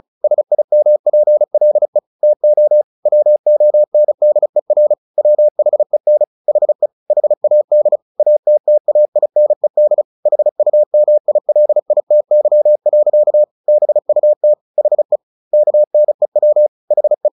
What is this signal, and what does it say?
Morse code, heard as a keyed tone, about 34 wpm, 600 Hz. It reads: SIMPLE TO WONDER WHEN HE HAD ATTAINED HUMILITY. BUT HE KNEW HE
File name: the old man and the sea / the old man and the sea 1 35wpm